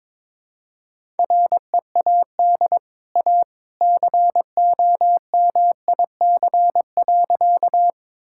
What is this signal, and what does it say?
Morse code, heard as a keyed tone, about 21 wpm, 700 Hz. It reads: READ A COMIC.
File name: practice / 22wpm / Easy Sentences with Hints - 22wpm